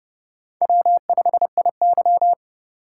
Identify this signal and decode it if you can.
Morse code, heard as a keyed tone, about 30 wpm, 700 Hz. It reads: W5IY
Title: USA Call Signs - 4 Characters - 2x Character Spacing - 30wpm